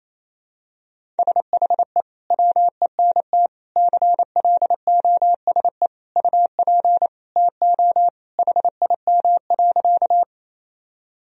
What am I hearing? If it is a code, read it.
Morse code, heard as a keyed tone, about 28 wpm, 700 Hz. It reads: SHE WENT CLOSE UP TO HIM.
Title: Sentences from Top 1000 Words - Review Part 1 - 28wpm